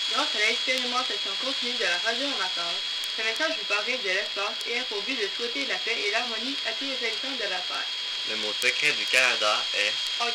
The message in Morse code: none